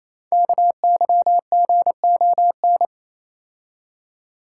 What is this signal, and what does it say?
Morse code, heard as a keyed tone, about 28 wpm, 700 Hz. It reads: KYGON
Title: Code Groups - 5 Letters - 28wpm